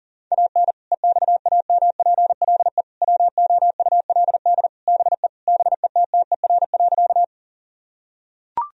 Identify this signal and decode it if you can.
Morse code, heard as a keyed tone, about 40 wpm, 700 Hz. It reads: AN EXAMPLE WOULD BE BETTER.
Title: Sentences from Top 1000 Words - Review Part 1 - 40wpm